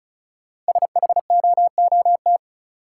Morse code, heard as a keyed tone, about 36 wpm, 700 Hz. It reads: SHOOT
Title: Sentences from Top 1200 Words - New Word Drill - 35wpm